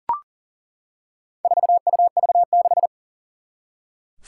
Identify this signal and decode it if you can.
Morse code, heard as a keyed tone, about 40 wpm, 700 Hz. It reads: 4UV6
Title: Four Challenging Letter-Number Combinations 40wpm